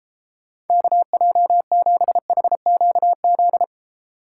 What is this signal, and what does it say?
Morse code, heard as a keyed tone, about 29 wpm, 700 Hz. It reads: KJ7HQZ